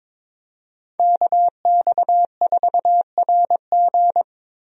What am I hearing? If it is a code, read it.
Morse code, heard as a keyed tone, about 22 wpm, 700 Hz. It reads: KX4RG
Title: USA Call Signs - 5 Characters - Repeated 2x - 22wpm